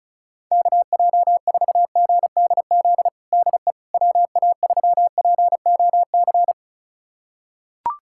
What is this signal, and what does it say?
Morse code, heard as a keyed tone, about 36 wpm, 700 Hz. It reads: KJ4GDZ DE WA3POC
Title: USA Callsign DE Callsign 35wpm